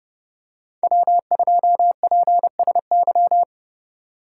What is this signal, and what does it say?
Morse code, heard as a keyed tone, about 28 wpm, 700 Hz. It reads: W2PSY